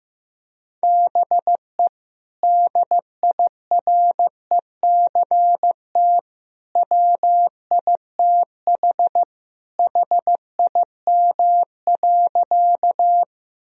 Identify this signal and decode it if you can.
Morse code, heard as a keyed tone, about 15 wpm, 700 Hz. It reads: BE DIRECT WITH HIM.
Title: Sentences from Top 1000 Words - Review Part 2 - 15wpm